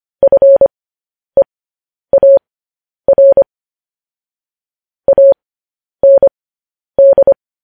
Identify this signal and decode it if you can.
Morse code, heard as a keyed tone, about 25 wpm, spaced as for 10 wpm, 550 Hz. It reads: FEAR AND
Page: News Headlines in Morse Code at 10 WPM